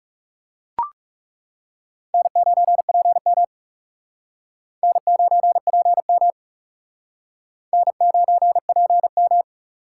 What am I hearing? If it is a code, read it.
Morse code, heard as a keyed tone, about 36 wpm, 700 Hz. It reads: N9PM N9PM N9PM
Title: Speed Racing - USA 1x2 Call Signs 30wpm